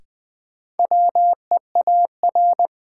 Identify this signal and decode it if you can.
Morse code, heard as a keyed tone, about 21 wpm, 700 Hz. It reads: WEAR
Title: Sentences from Top 1000 Words - Review Part 6 - New Word Drill - 20wpm